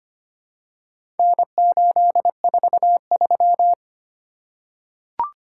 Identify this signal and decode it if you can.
Morse code, heard as a keyed tone, about 25 wpm, 700 Hz. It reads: N843
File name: practice / 25wpm / Four Letter-Number Combinations - 25wpm